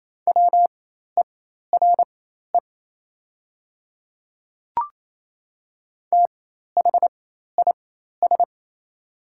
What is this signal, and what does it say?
Morse code, heard as a keyed tone, about 28 wpm, spaced as for 13 wpm, 700 Hz. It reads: WERE THIS
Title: 4x Character Spacing - Top 100 Words - 28wpm